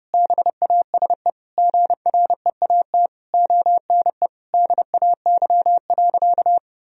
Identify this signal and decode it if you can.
Morse code, heard as a keyed tone, about 30 wpm, 700 Hz. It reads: BASE GREAT ONE DAY.
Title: Sentences from Top 1000 Words - Review Part 1 - 30wpm